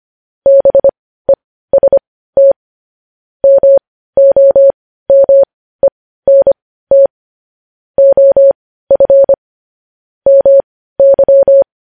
Morse code, heard as a keyed tone, about 25 wpm, spaced as for 15 wpm, 550 Hz. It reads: BEST MOMENT OF MY
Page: news headlines in morse code at 15 wpm